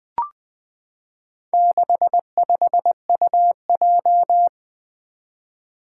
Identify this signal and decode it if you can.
Morse code, heard as a keyed tone, about 20 wpm, 700 Hz. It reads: 65UJ
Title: Four Challenging Letter-Number Combinations 20wpm